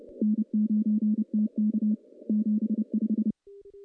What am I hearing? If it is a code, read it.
Morse code, heard as a keyed tone, about 31 wpm, 210 Hz. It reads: N9TK 75
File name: contests / CW-SS-k5tr- -1300